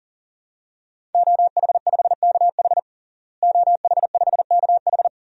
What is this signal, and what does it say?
Morse code, heard as a keyed tone, about 40 wpm, 700 Hz. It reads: OH5KH OH5KH
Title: DXCC-100 Call Signs - 5 Characters - Repeated 2x - 40wpm